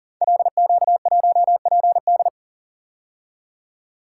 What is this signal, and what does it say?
Morse code, heard as a keyed tone, about 42 wpm, 700 Hz. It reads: LQ1PD
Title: Code Groups - 5 Letter:Numbers - 40wpm